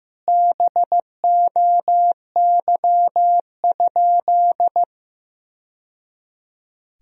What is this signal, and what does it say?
Morse code, heard as a keyed tone, about 15 wpm, 700 Hz. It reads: BOY?
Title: Sentences from Top 1000 Words - Review Part 1 - 15wpm